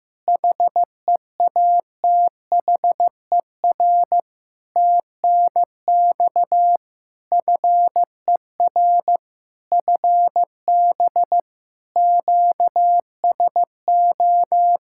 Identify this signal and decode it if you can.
Morse code, heard as a keyed tone, about 15 wpm, 700 Hz. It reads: HEATHER TNX FER FB QSO
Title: K1USN SST QSO Practice 15wpm